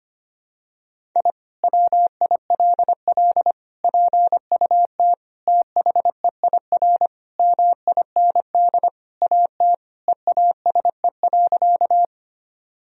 Morse code, heard as a keyed tone, about 25 wpm, 700 Hz. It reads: I WILL PUT THEIR MIND AT EASE.
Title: Sentences from Top 1000 Words - Review Part 1 - 25wpm